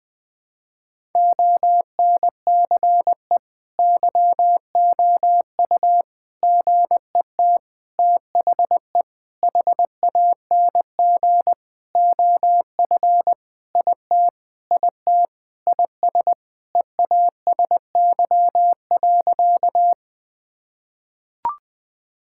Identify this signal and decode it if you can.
Morse code, heard as a keyed tone, about 21 wpm, 700 Hz. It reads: ONCE YOU GET THE HANG OF IT IT IS EASY.
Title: Sentences with 7-Word Phrases - 20wpm